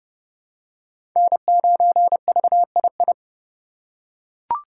Morse code, heard as a keyed tone, about 30 wpm, 700 Hz. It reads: N9VII
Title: USA Call Signs - 5 Characters - 2x Character Spacing - 30wpm